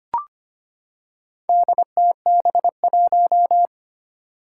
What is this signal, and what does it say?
Morse code, heard as a keyed tone, about 25 wpm, 700 Hz. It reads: DTB1
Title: Four Challenging Letter-Number Combinations 25wpm